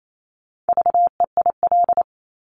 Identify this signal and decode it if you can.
Morse code, heard as a keyed tone, about 28 wpm, 700 Hz. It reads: VEIL
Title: Multi-letter Phonemes Repeated 3x 28wpm